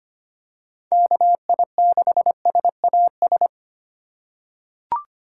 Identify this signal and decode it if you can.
Morse code, heard as a keyed tone, about 25 wpm, 700 Hz. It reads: KI6SAS
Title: USA Call Signs - 6 Characters - 2x Character Spacing - 25wpm